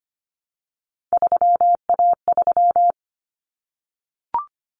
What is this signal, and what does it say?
Morse code, heard as a keyed tone, about 25 wpm, 700 Hz. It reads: 3A3
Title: Common Cut Numbers - 3-Digit 25wpm